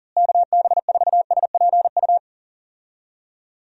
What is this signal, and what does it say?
Morse code, heard as a keyed tone, about 40 wpm, 700 Hz. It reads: KB4SPU
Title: USA Call Signs - 6 Characters - Repeated 3x - 40wpm